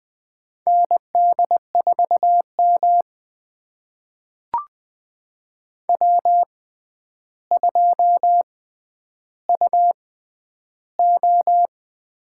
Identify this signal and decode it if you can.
Morse code, heard as a keyed tone, about 20 wpm, spaced as for 10 wpm, 700 Hz. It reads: ND4M W2UO
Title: USA Call Signs - 6x Character Spacing - 20wpm